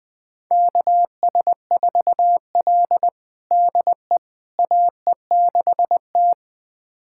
Morse code, heard as a keyed tone, about 20 wpm, 700 Hz. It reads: KS4L DE AE6T
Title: USA Callsign DE Callsign - 4 Characters - 20wpm